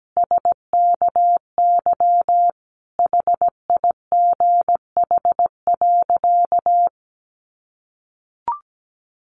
Morse code, heard as a keyed tone, about 17 wpm, 700 Hz. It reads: SKY HIGH.
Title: Sentences from Top 1200 Words 17wpm